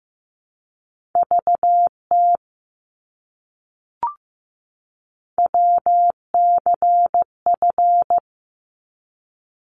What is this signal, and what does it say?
Morse code, heard as a keyed tone, about 15 wpm, 700 Hz. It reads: VT WCF
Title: ARRL Sections 15wpm